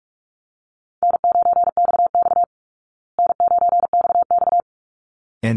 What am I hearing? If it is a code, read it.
Morse code, heard as a keyed tone, about 44 wpm, 700 Hz. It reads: N9XX N9XX
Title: Easy Call Signs Repeated 2x - 45wpm